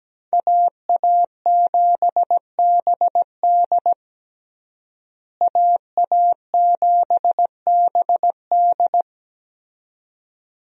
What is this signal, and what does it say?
Morse code, heard as a keyed tone, about 17 wpm, 700 Hz. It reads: AA7BD AA7BD
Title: USA Call Signs - 5 Characters - Repeated 2x - 17wpm